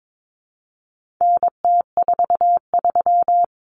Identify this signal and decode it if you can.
Morse code, heard as a keyed tone, about 22 wpm, 700 Hz. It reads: NT43
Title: Common Cut Numbers - 4-Digit 22wpm